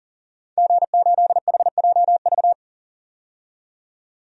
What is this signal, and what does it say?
Morse code, heard as a keyed tone, about 40 wpm, 700 Hz. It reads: G8HJV
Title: DXCC-100 Call Signs - 5 Characters - 40wpm